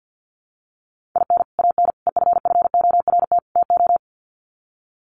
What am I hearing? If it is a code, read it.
Morse code, heard as a keyed tone, about 49 wpm, 700 Hz. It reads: IN AN EFFORT TO
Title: Sets of 4 Words 50wpm